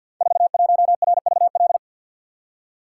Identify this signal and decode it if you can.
Morse code, heard as a keyed tone, about 47 wpm, 700 Hz. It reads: 41RVL
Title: Code Groups - 5 Letter:Numbers - 50wpm